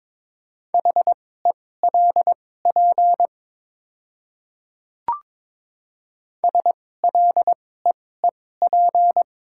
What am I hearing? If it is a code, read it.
Morse code, heard as a keyed tone, about 22 wpm, spaced as for 16 wpm, 690 Hz. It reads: HELP SLEEP
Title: 2x Character Spacing - Top 500 Words - 22wpm